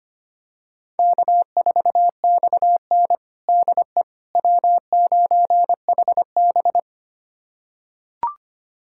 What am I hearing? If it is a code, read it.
Morse code, heard as a keyed tone, about 25 wpm, 700 Hz. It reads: K4XN DE W9HB
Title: USA Callsign DE Callsign - 4 Characters - 25wpm